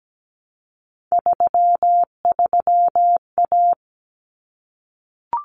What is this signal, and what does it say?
Morse code, heard as a keyed tone, about 17 wpm, 700 Hz. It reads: 33A
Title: Common Cut Numbers - 3-Digit 17wpm